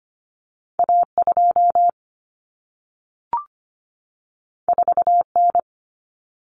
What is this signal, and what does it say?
Morse code, heard as a keyed tone, about 25 wpm, 700 Hz. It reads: A2 4N
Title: Common Cut Numbers - 2-Digit 25wpm